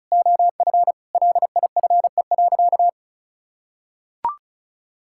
Morse code, heard as a keyed tone, about 34 wpm, 700 Hz. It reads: OF LIFE.
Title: Sentences from Top 1000 Words - Review Part 6 - 35wpm